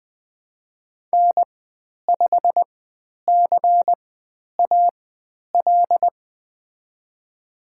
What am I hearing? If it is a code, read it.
Morse code, heard as a keyed tone, about 20 wpm, spaced as for 10 wpm, 700 Hz. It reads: N5CAL
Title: 095 Lesson 16g - Letter c - Call Signs 20wpm